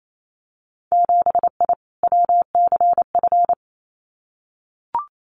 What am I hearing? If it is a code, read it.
Morse code, heard as a keyed tone, about 28 wpm, 700 Hz. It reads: 7I WCF
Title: Winter Field Day Exchanges Encore 28wpm